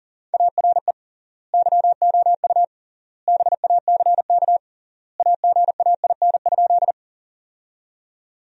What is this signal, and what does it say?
Morse code, heard as a keyed tone, about 37 wpm, 700 Hz. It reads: ARE YOU BACK AGAIN?
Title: 3x Word-Spacing - Sentences from Top 200 Words - 40wpm